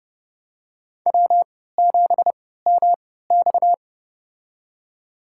Morse code, heard as a keyed tone, about 30 wpm, spaced as for 17 wpm, 700 Hz. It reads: W7MX
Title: USA Call Signs - 4 Characters - 3x Character Spacing - 30wpm